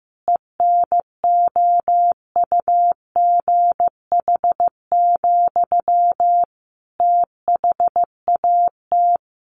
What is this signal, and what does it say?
Morse code, heard as a keyed tone, about 15 wpm, 700 Hz. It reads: ENOUGH, THAT
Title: Sentences from Top 800 Words 15wpm